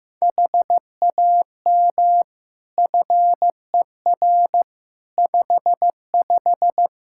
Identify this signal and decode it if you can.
Morse code, heard as a keyed tone, about 15 wpm, 700 Hz. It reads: HAM FER 55
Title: K1USN SST QSO Practice 15wpm